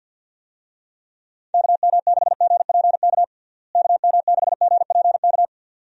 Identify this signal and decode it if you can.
Morse code, heard as a keyed tone, about 51 wpm, 700 Hz. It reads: KM6GPK KM6GPK
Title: USA Call Signs - 6 Characters - Repeated 2x - 50wpm